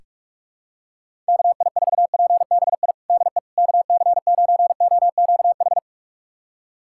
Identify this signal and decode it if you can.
Morse code, heard as a keyed tone, about 44 wpm, 700 Hz. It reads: KI4PBI DE KC9OQH
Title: USA Callsign DE Callsign 45wpm